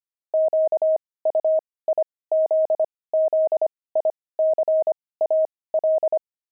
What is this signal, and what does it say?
Morse code, heard as a keyed tone, about 25 wpm, spaced as for 18 wpm, 630 Hz. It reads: QUIZZICAL
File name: pangram-fs-18-cs-25-jocks